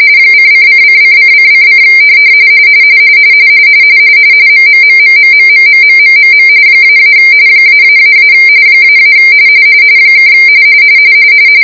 RTTY, about 45.45 baud. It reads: ::::::::: MMMMMMMMMMMM -- ---- -- (